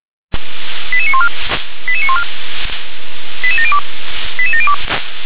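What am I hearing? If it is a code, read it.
Five-tone selective call (ZVEI), heard as quick runs of five tones, tones of about 70 ms. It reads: 88914 88914 88702 88702
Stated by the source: Five-tone ZVEI selective call received by SDR.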